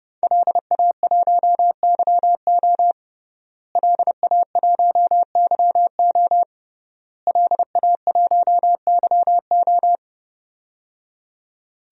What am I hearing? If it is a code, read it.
Morse code, heard as a keyed tone, about 30 wpm, 700 Hz. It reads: LA1YO LA1YO LA1YO
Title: DXCC-100 Call Signs - 5 Characters - Repeated 3x - 30wpm